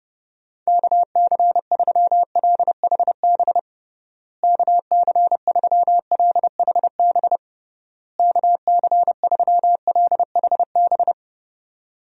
Morse code, heard as a keyed tone, about 30 wpm, 700 Hz. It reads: KC3LHB KC3LHB KC3LHB
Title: USA Call Signs - 6 Characters - Repeated 3x - 30wpm